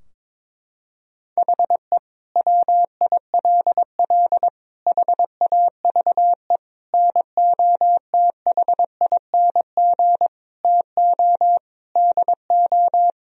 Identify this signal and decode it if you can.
Morse code, heard as a keyed tone, about 22 wpm, 700 Hz. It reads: HE WILL HAVE NOTHING TO DO
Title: Sentences from Top 1000 Words - Review Part 4 - 22wpm